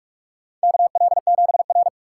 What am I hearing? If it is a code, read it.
Morse code, heard as a keyed tone, about 46 wpm, 700 Hz. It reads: KL7R
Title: USA Call Signs - 4 Characters - 4x Character Spacing - 45wpm